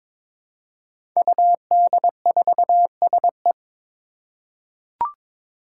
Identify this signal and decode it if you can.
Morse code, heard as a keyed tone, about 21 wpm, 700 Hz. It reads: UD4SE
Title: Five Challenging Letter-Number Combinations 22wpm